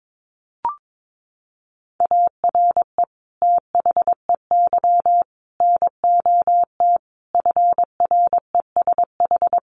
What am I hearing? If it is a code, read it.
Morse code, heard as a keyed tone, about 22 wpm, 700 Hz. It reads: ARE THEY NOT FRESH
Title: Sentences from Top 1200 Words 22wpm